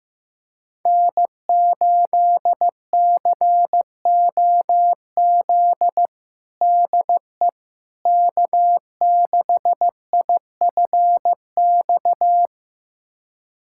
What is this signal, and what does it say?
Morse code, heard as a keyed tone, about 15 wpm, 700 Hz. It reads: N8COZ DE K6IFX
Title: USA Callsign DE Callsign - 5 Characters - 15wpm